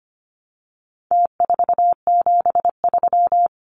Morse code, heard as a keyed tone, about 25 wpm, 700 Hz. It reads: T473
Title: Numbers - Contest Style - 4-Digit 25wpm